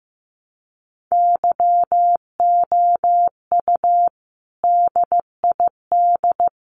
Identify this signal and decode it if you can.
Morse code, heard as a keyed tone, about 15 wpm, 700 Hz. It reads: YOU DID
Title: Sets of 3 Words 15wpm